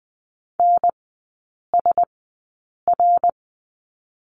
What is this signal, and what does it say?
Morse code, heard as a keyed tone, about 20 wpm, spaced as for 8 wpm, 700 Hz. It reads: NSR
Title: 053 Lesson 10e - Letter r - Three Characters 20wpm